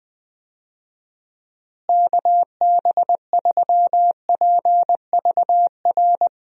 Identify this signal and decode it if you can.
Morse code, heard as a keyed tone, about 21 wpm, 700 Hz. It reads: KB3PVR
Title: USA Call Signs - 6 Characters - Repeated 2x - 20wpm